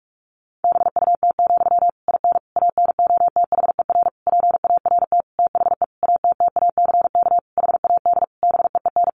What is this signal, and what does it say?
Morse code, heard as a keyed tone, about 44 wpm, 700 Hz. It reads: BUT, IN ANOTHER PART THE ATTACK HAD BEEN
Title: Sentences from Top 1500 Words 45wpm